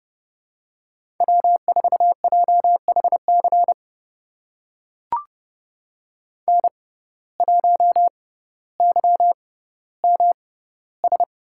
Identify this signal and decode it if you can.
Morse code, heard as a keyed tone, about 30 wpm, spaced as for 15 wpm, 700 Hz. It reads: W4JHC N1YMS
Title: USA Call Signs - 5 Characters - 6x Character Spacing - 30wpm